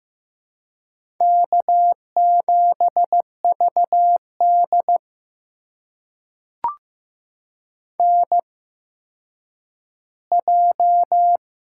Morse code, heard as a keyed tone, about 15 wpm, spaced as for 9 wpm, 700 Hz. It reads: K7VD NJ
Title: USA Call Signs - 4 Characters - 8x Character Spacing - 15wpm